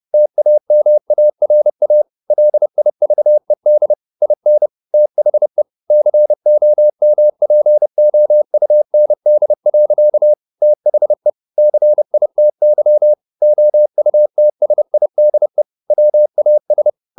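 Morse code, heard as a keyed tone, about 30 wpm, 600 Hz. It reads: TAMARA LIVED IN THE COMPOUND. THE CITY OUTSIDE WAS
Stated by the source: Morse code at 29 wpm, sending Ken Follett's novel Never in English, from CW med Gnister.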